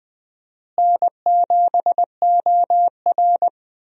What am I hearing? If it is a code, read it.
Morse code, heard as a keyed tone, about 20 wpm, 700 Hz. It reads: N7OR
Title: USA Call Signs - 4 Characters - Repeated 3x - 20wpm